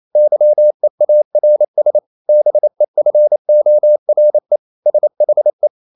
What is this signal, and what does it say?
Morse code, heard as a keyed tone, about 28 wpm, 600 Hz. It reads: YEARS BEFORE SHE
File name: Never 0027wpm